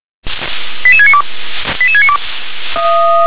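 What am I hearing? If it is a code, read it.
Five-tone selective call (ZVEI), heard as quick runs of five tones, tones of about 70 ms. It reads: 88692 88692
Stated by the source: ZVEI five-tone selective call received by SDR.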